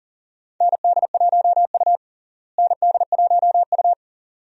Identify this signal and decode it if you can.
Morse code, heard as a keyed tone, about 40 wpm, 700 Hz. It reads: ND1U ND1U
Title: USA Call Signs - 4 Characters - Repeated 2x - 40wpm